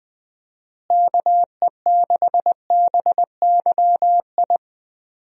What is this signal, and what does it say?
Morse code, heard as a keyed tone, about 20 wpm, 700 Hz. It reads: KE6BYI
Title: USA Call Signs - 6 Characters - Repeated 3x - 20wpm